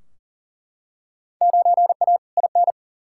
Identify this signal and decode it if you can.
Morse code, heard as a keyed tone, about 40 wpm, 700 Hz. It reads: 9A IN
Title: Field Day Exchanges Encore - 40wpm